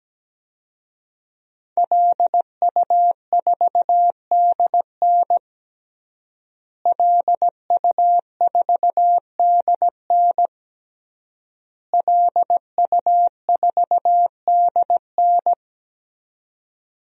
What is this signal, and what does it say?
Morse code, heard as a keyed tone, about 17 wpm, 700 Hz. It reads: LU4DN LU4DN LU4DN
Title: DXCC-100 Call Signs - 5 Characters - Repeated 3x - 17wpm